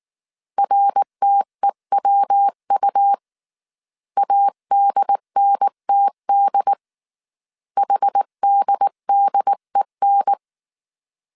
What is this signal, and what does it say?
Morse code, heard as a keyed tone, about 19 wpm, 800 Hz. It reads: RTEWU ADNTD HDDEN